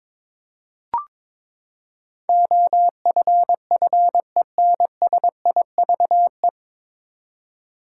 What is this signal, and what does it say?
Morse code, heard as a keyed tone, about 21 wpm, 700 Hz. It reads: OFFENSIVE
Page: Morse Code Ninja practice 22wpm